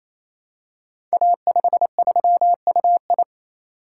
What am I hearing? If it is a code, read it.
Morse code, heard as a keyed tone, about 29 wpm, 700 Hz. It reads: A53UI
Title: Code Groups - 5 Letter:Numbers - 28wpm